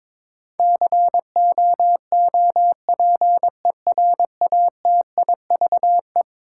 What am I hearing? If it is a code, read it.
Morse code, heard as a keyed tone, about 21 wpm, 700 Hz. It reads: COOPERATIVE
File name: practice / 22wpm / Eleven Letter Words - 22wpm